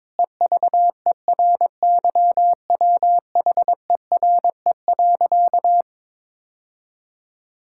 Morse code, heard as a keyed tone, about 22 wpm, 700 Hz. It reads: EVERYWHERE.
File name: practice / 22wpm / Sentences with 8-Word Phrases - 22wpm